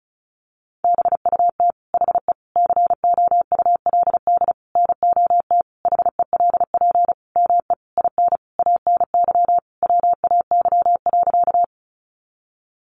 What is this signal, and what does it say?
Morse code, heard as a keyed tone, about 34 wpm, 700 Hz. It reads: BUT HE COULD NOT HELP ME IN ANY WAY.
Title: Sentences from Top 600 Words 35wpm